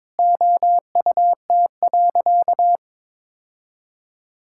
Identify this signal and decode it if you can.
Morse code, heard as a keyed tone, about 21 wpm, 700 Hz. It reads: OUT.
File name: practice / 22wpm / Sentences with 8-Word Phrases - 22wpm